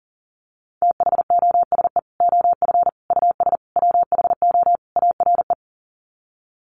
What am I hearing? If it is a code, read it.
Morse code, heard as a keyed tone, about 40 wpm, 700 Hz. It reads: THOSE OF US WHO ARE
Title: Sets of 5 Words Encore2 40wpm